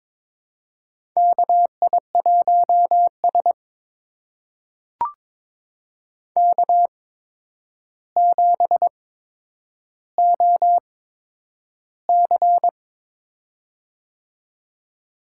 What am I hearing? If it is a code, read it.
Morse code, heard as a keyed tone, about 22 wpm, spaced as for 9 wpm, 700 Hz. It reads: KI1S K7OC
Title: USA Call Signs - 4 Characters - 8x Character Spacing - 22wpm